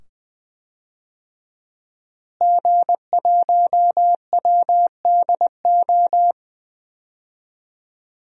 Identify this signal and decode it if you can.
Morse code, heard as a keyed tone, about 20 wpm, 700 Hz. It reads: G1WDO
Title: DXCC-100 Call Signs - 5 Characters - Repeated 2x - 20wpm